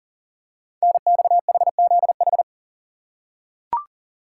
Morse code, heard as a keyed tone, about 38 wpm, 700 Hz. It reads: NXHZH
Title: Code Groups - 5 Letters - 40wpm